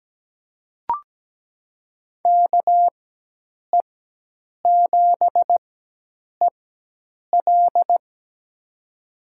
Morse code, heard as a keyed tone, about 17 wpm, spaced as for 8 wpm, 700 Hz. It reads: KE7EL
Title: USA Call Signs - 5 Characters - 4x Character Spacing - 17wpm